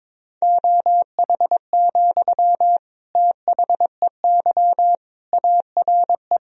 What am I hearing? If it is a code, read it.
Morse code, heard as a keyed tone, about 22 wpm, 700 Hz. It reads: OH, THEY ARE
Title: Sentences from Top 1000 Words - Review Part 4 - 22wpm